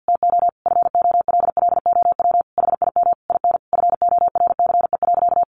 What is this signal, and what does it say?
Morse code, heard as a keyed tone, about 50 wpm, 700 Hz. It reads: TO FOLLOW HIM IN FORCE.
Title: Sentences from Top 700 Words 50wpm